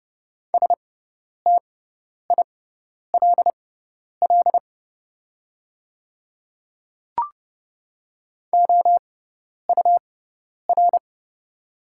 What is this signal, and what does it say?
Morse code, heard as a keyed tone, about 29 wpm, spaced as for 10 wpm, 700 Hz. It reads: STILL OUR
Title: 6x Character Spacing - Top 300 Words - 30wpm